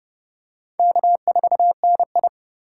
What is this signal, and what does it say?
Morse code, heard as a keyed tone, about 26 wpm, 700 Hz. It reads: K4NI